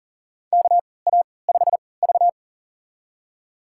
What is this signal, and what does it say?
Morse code, heard as a keyed tone, about 40 wpm, spaced as for 23 wpm, 700 Hz. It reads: KA5V
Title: USA Call Signs - 4 Characters - 3x Character Spacing - 40wpm